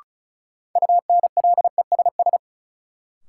Morse code, heard as a keyed tone, about 36 wpm, 700 Hz. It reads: UNLESS
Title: Sentences from Top 1000 Words - Review Part 5 - New Word Drill - 35wpm